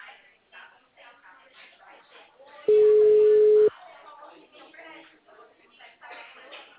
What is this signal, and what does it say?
Ringback tone (Europe, 425 Hz), with speech.
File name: out- -2009- - - .321